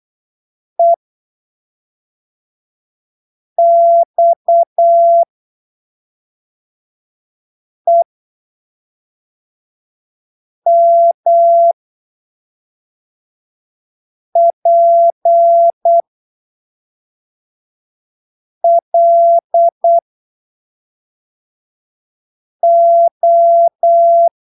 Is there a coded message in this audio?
Yes: Morse code from 0.8 to 24.3 s, heard as a keyed tone, about 8 wpm, spaced as for 3 wpm, 680 Hz. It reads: EXEMPLO